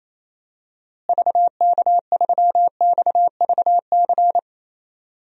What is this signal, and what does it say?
Morse code, heard as a keyed tone, about 28 wpm, 700 Hz. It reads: VK3XVC
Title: DXCC-100 Call Signs - 6 Characters - Repeated 3x - 28wpm